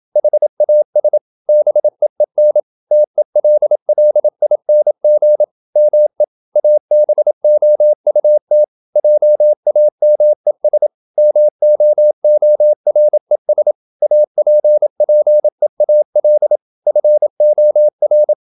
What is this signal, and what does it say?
Morse code, heard as a keyed tone, about 27 wpm, 600 Hz. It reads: HAS BEEN TELLING ME ABOUT JAMES MOORES APPEAL FOR
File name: Never 0027wpm